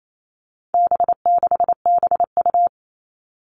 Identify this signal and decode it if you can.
Morse code, heard as a keyed tone, about 28 wpm, 700 Hz. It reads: B6BU
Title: All Cut Numbers - 4-Digit 28wpm